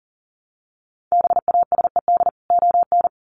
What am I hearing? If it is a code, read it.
Morse code, heard as a keyed tone, about 40 wpm, 700 Hz. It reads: BASED ON
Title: Sets of 2 Words 40wpm